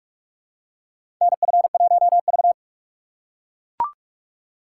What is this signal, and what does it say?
Morse code, heard as a keyed tone, about 41 wpm, 690 Hz. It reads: NF1V
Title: USA Call Signs - 4 Characters - 4x Character Spacing - 45wpm